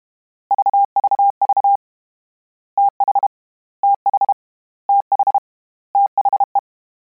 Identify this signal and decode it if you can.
Morse code, heard as a keyed tone, about 32 wpm, 800 Hz. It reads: VVV TH TH TH THE
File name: Combinations / Combinations-mp3s / Combinations@30